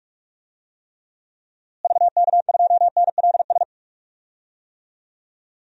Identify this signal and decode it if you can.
Morse code, heard as a keyed tone, about 45 wpm, 700 Hz. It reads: VK2NLS
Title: DXCC-100 Call Signs - 6 Characters - Repeated 3x - 45wpm